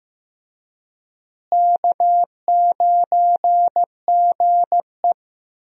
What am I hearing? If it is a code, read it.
Morse code, heard as a keyed tone, about 15 wpm, 700 Hz. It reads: K9GE
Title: USA Call Signs - 4 Characters - Repeated 3x - 15wpm